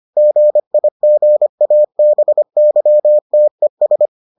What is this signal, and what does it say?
Morse code, heard as a keyed tone, about 25 wpm, 600 Hz. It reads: GIGABYTES